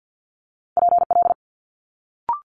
Morse code, heard as a keyed tone, about 50 wpm, 700 Hz. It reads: PL
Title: Multi-letter Phonemes Quick Fire 50wpm